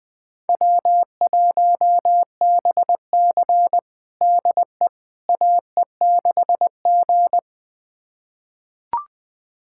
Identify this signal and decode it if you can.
Morse code, heard as a keyed tone, about 20 wpm, 700 Hz. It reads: W1BC DE AE6G
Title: USA Callsign DE Callsign - 4 Characters - 20wpm